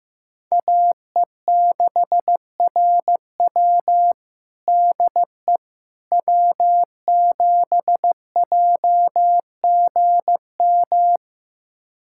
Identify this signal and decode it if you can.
Morse code, heard as a keyed tone, about 15 wpm, 700 Hz. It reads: AE6RW DE W7JGM
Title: USA Callsign DE Callsign - 5 Characters - 15wpm